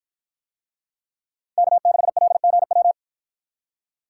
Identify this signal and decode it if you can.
Morse code, heard as a keyed tone, about 52 wpm, 700 Hz. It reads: K6LGW